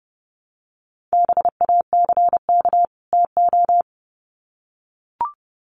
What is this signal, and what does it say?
Morse code, heard as a keyed tone, about 30 wpm, 700 Hz. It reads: BACK TO
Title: Sets of 2 Words 30wpm